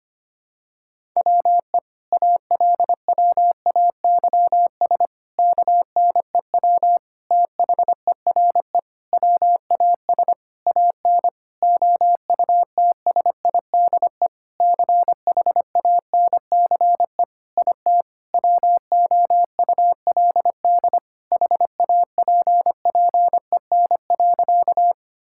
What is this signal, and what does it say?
Morse code, heard as a keyed tone, about 25 wpm, 700 Hz. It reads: WE ALWAYS KNEW THERE WAS AN OUTSIDE CHANCE IT WOULD HAPPEN.
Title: Sentences from Top 1000 Words - Review Part 6 - 25wpm